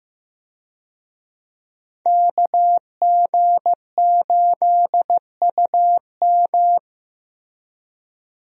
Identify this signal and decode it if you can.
Morse code, heard as a keyed tone, about 15 wpm, 700 Hz. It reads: KG8UM
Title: USA Call Signs - 5 Characters - Repeated 2x - 15wpm